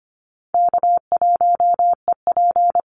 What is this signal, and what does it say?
Morse code, heard as a keyed tone, about 25 wpm, 700 Hz. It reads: K1EP
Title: Easy Call Signs Repeated 2x - 25wpm